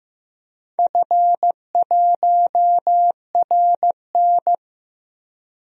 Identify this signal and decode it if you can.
Morse code, heard as a keyed tone, about 15 wpm, 700 Hz. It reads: F1RN
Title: DXCC-100 Call Signs - 4 Characters - Repeated 2x - 15wpm